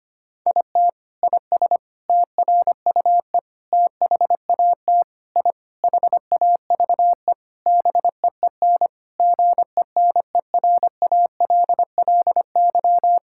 Morse code, heard as a keyed tone, about 25 wpm, 700 Hz. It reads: IT IS TRUE THAT I HAVE BEEN GENERALLY
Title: Sentences from Top 1000 Words - Review Part 6 - 25wpm